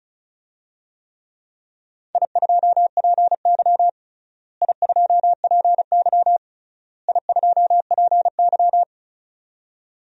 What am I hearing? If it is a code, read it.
Morse code, heard as a keyed tone, about 34 wpm, 700 Hz. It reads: I2PY I2PY I2PY
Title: DXCC-100 Call Signs - 4 Characters - Repeated 3x - 35wpm